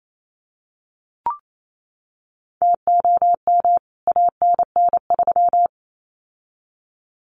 Morse code, heard as a keyed tone, about 28 wpm, 700 Hz. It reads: TOM ANN3
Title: CWT Exchange 28wpm